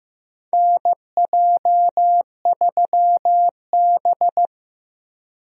Morse code, heard as a keyed tone, about 15 wpm, 700 Hz. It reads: NJ3B
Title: USA Call Signs - 4 Characters - Repeated 2x - 15wpm